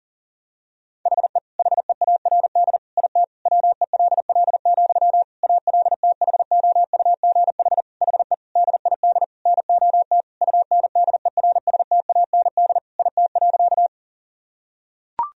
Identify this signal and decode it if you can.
Morse code, heard as a keyed tone, about 40 wpm, 700 Hz. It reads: HE HEARD IT WELL, ALTHOUGH HE DID NOT UNDERSTAND IT.
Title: Sentences from Top 1000 Words - Review Part 3 - 40wpm